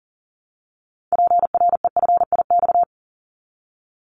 Morse code, heard as a keyed tone, about 40 wpm, 700 Hz. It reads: PREFIX
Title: Callsign Prefixes for Top 50 DXCC - 40wpm